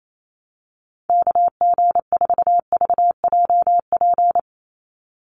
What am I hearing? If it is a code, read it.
Morse code, heard as a keyed tone, about 28 wpm, 700 Hz. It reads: KG4VJP
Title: USA Call Signs 28wpm